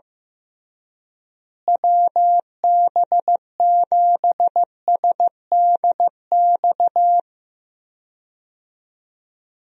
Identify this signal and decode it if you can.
Morse code, heard as a keyed tone, about 15 wpm, 700 Hz. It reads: WB7SDX
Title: USA Call Signs - 6 Characters - Repeated 3x - 15wpm